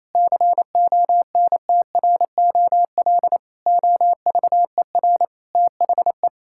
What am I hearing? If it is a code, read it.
Morse code, heard as a keyed tone, about 27 wpm, 700 Hz. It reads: CONTROL OVER THE
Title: Sentences from Top 1000 Words - Review Part 5 - 28wpm